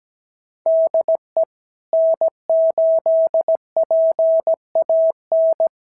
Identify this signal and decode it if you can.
Morse code, heard as a keyed tone, about 17 wpm, 650 Hz. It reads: DE N8PAN
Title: SOTA QSO Practice 17wpm